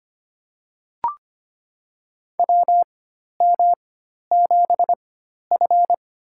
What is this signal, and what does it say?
Morse code, heard as a keyed tone, about 25 wpm, spaced as for 12 wpm, 700 Hz. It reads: WM7F
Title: USA Call Signs - 4x Character Spacing - 25wpm